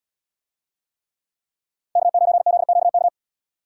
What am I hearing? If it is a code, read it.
Morse code, heard as a keyed tone, about 71 wpm, 700 Hz. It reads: K9ZQK